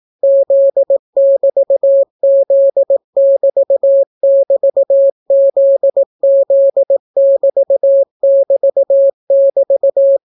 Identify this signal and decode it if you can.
Morse code, heard as a keyed tone, about 18 wpm, 550 Hz. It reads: Z<BT>Z<BT><BT>ZZ<BT><BT><BT>